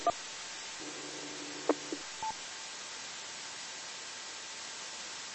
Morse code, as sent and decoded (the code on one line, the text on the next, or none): none